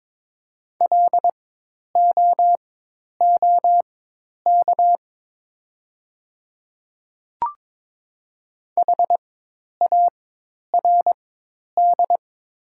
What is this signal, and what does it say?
Morse code, heard as a keyed tone, about 23 wpm, spaced as for 10 wpm, 690 Hz. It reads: LOOK HARD
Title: 4x Character Spacing - Top 300 Words - 22wpm